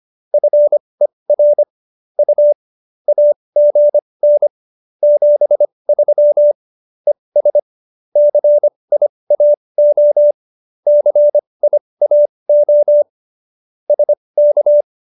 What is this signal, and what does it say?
Morse code, heard as a keyed tone, about 25 wpm, spaced as for 20 wpm, 590 Hz. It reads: FER U AGN 73 ES CIAO CIAO SK
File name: qso215b 20